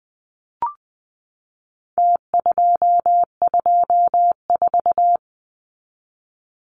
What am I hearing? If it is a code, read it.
Morse code, heard as a keyed tone, about 20 wpm, 700 Hz. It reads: T224
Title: Numbers - Contest Style - 4-Digit 20wpm